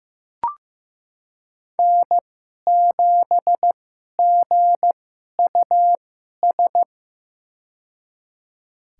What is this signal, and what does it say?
Morse code, heard as a keyed tone, about 15 wpm, spaced as for 11 wpm, 700 Hz. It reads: N7GUS
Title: USA Call Signs - 5 Characters - 2x Character Spacing - 15wpm